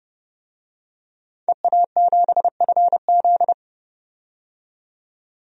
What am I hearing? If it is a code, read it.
Morse code, heard as a keyed tone, about 31 wpm, 700 Hz. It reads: EA7FZ
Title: DXCC-100 Call Signs - 5 Characters - 30wpm